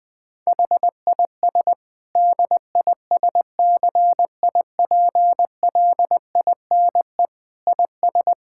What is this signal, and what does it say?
Morse code, heard as a keyed tone, about 20 wpm, 700 Hz. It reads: HIS DISCIPLINE IS
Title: Sentences from Top 1000 Words - Review Part 6 - 20wpm